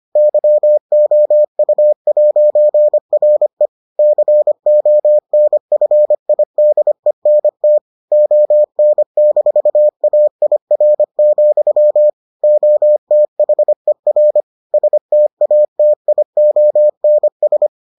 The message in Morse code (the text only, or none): YOU'RE CONFIDENT ON-AIR, OTHER STATIONS